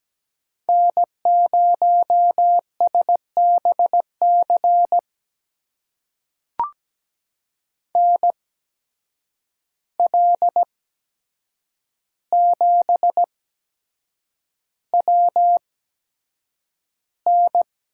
Morse code, heard as a keyed tone, about 17 wpm, spaced as for 7 wpm, 700 Hz. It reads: N0SBC NL7WN
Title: USA Call Signs - 5 Characters - 8x Character Spacing - 17wpm